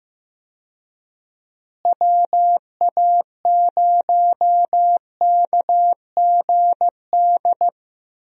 Morse code, heard as a keyed tone, about 15 wpm, 700 Hz. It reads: WA0KGD
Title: USA Call Signs - 6 Characters - Repeated 2x - 15wpm